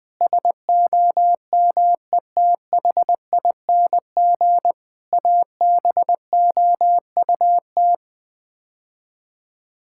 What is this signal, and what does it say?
Morse code, heard as a keyed tone, about 20 wpm, 700 Hz. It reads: SOMETHING ABOUT
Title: Build A Phrase - 5-Word Phrases - 20wpm